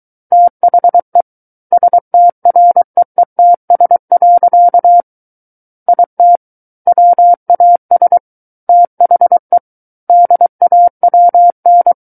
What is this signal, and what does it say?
Morse code, heard as a keyed tone, about 23 wpm, 700 Hz. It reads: THE STREETS. IT WAS THE DAWN
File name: Book One / 14-Chapter 14 - 23 WPM